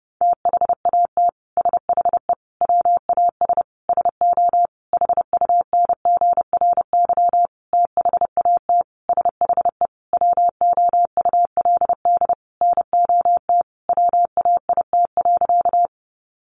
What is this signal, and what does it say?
Morse code, heard as a keyed tone, about 30 wpm, 700 Hz. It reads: THAT SHE WAS SO HUNGRY THAT SHE WOULD NOT WAIT.
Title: Sentences from Top 1400 Words 30wpm